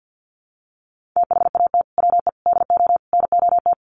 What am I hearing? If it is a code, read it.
Morse code, heard as a keyed tone, about 50 wpm, 700 Hz. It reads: THAT WE DO NOT
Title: Sets of 4 Words 50wpm